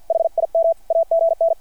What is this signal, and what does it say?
Morse code, heard as a keyed tone, about 50 wpm, 660 Hz. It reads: HIM AGN
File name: archived / CRSnet